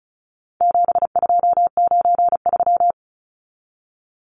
Morse code, heard as a keyed tone, about 34 wpm, 700 Hz. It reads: 7293
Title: Numbers - 4-Digit 35wpm